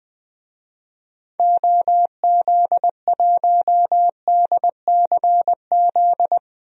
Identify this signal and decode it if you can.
Morse code, heard as a keyed tone, about 20 wpm, 700 Hz. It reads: OZ1DCZ
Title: DXCC-100 Call Signs - 6 Characters - Repeated 2x - 20wpm